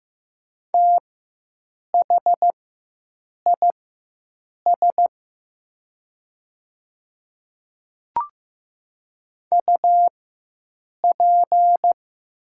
Morse code, heard as a keyed tone, about 15 wpm, spaced as for 7 wpm, 700 Hz. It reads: THIS UP
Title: 4x Character Spacing - Top 300 Words - 15wpm